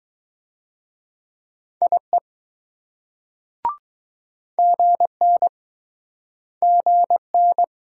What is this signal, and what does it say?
Morse code, heard as a keyed tone, about 20 wpm, 700 Hz. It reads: IE GN GN